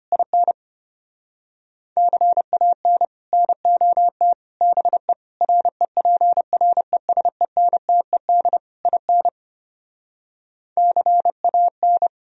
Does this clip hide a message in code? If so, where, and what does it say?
Yes: Morse code from 0.1 to 12.1 s, heard as a keyed tone, about 33 wpm, 700 Hz. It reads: IN CAN NOT BE REPRESENTED IN CAN